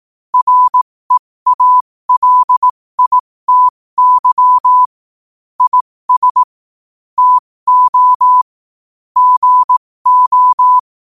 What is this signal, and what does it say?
Morse code, heard as a keyed tone, about 18 wpm, spaced as for 15 wpm, 1000 Hz. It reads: REALITY IS TO GO
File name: qotdfiles / 15wpm